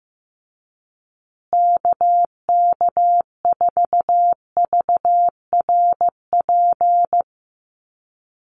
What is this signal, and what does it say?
Morse code, heard as a keyed tone, about 15 wpm, 700 Hz. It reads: KK4VRP
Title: USA Call Signs Repeated 3x 15wpm